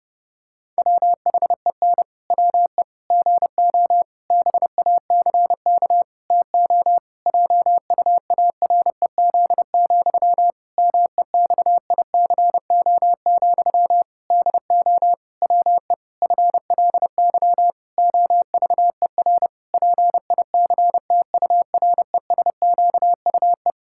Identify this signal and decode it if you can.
Morse code, heard as a keyed tone, about 30 wpm, 700 Hz. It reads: WHEN WE GO BACK TO JUAREZ, MEXICO, DO WE FLY OVER PICTURESQUE
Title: Pangram Challenge v2 - 30wpm